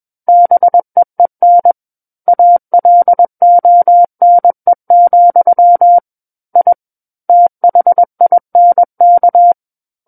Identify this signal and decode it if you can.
Morse code, heard as a keyed tone, about 21 wpm, 700 Hz. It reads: BEEN ALONE, I THINK
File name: Book One / 12-Chapter 12 - 21 WPM